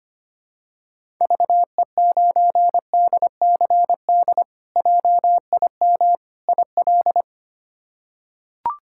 Morse code, heard as a keyed tone, about 25 wpm, 700 Hz. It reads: VE9DCD JIM IL
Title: K1USN SST QSO Practice - 25wpm